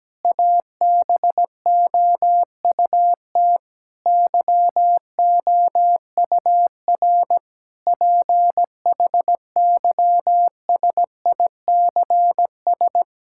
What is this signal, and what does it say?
Morse code, heard as a keyed tone, about 17 wpm, 700 Hz. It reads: ABOUT YOUR PHYSICS
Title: Sentences from Top 1000 Words - Review Part 5 - 17wpm